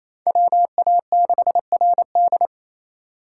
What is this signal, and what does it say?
Morse code, heard as a keyed tone, about 36 wpm, 700 Hz. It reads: WA6RD